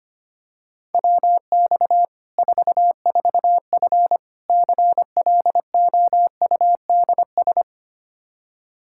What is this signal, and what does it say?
Morse code, heard as a keyed tone, about 25 wpm, 700 Hz. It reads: WX 44F CLOUDS
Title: K1USN SST QSO Practice 25wpm